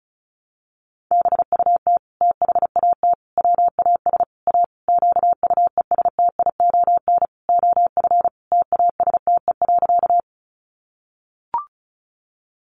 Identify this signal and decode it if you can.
Morse code, heard as a keyed tone, about 34 wpm, 700 Hz. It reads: BUT THAT WAS A QUESTION OF TASTE.
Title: Sentences from Top 1100 Words 35wpm